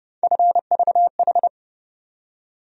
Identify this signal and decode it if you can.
Morse code, heard as a keyed tone, about 31 wpm, 700 Hz. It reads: FVH
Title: USA Call Sign Postfixes - 3 Letters 30wpm